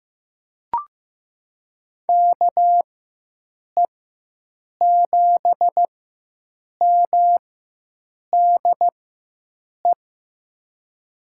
Morse code, heard as a keyed tone, about 15 wpm, spaced as for 7 wpm, 700 Hz. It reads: KE7MDE
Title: USA Call Signs - 6 Characters - 4x Character Spacing - 15wpm